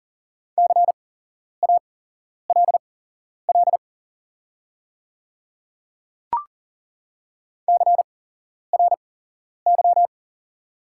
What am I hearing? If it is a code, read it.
Morse code, heard as a keyed tone, about 40 wpm, spaced as for 11 wpm, 690 Hz. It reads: CALL CRY
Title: 8x Character Spacing - Top 500 Words - 40wpm